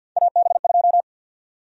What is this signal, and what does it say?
Morse code, heard as a keyed tone, about 49 wpm, 700 Hz. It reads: AB2
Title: USA Call Sign Prefixes - 50wpm